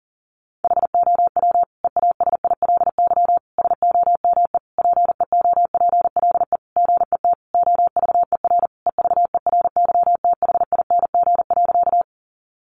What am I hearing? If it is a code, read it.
Morse code, heard as a keyed tone, about 40 wpm, 700 Hz. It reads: HOW EASILY SOME PEOPLE GET OVER EVERYTHING.
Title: Sentences from Top 900 Words 40wpm